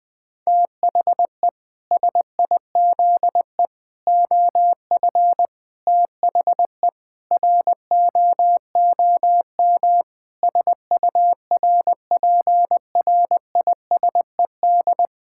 Morse code, heard as a keyed tone, about 20 wpm, 700 Hz. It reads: THE SIZE OF THE ROOM SURPRISED
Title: Sentences with 4-Word Phrases - 20wpm